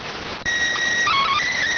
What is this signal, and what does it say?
SSTV (slow-scan television), PD 290.